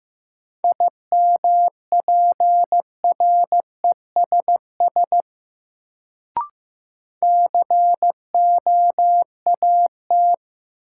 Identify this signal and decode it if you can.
Morse code, heard as a keyed tone, about 15 wpm, 700 Hz. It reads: IMPRESS COAT
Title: Sentences from Top 1600 Words - New Word Drill Alt - 15wpm